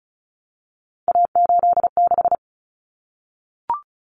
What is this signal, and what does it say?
Morse code, heard as a keyed tone, about 34 wpm, 700 Hz. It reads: A86
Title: Common Cut Numbers - 3-Digit 35wpm